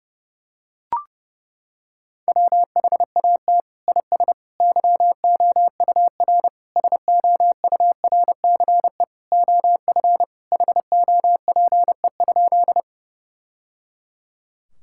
Morse code, heard as a keyed tone, about 30 wpm, 700 Hz. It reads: WHAT IS YOUR SOURCE OF HOPE?
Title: Sentences from Top 1000 Words - Review Part 5 - 30wpm